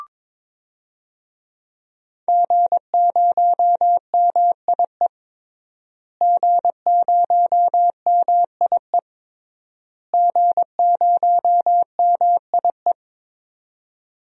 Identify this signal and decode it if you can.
Morse code, heard as a keyed tone, about 22 wpm, 700 Hz. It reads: G0MIE G0MIE G0MIE
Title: DXCC-100 Call Signs - 5 Characters - Repeated 3x - 22wpm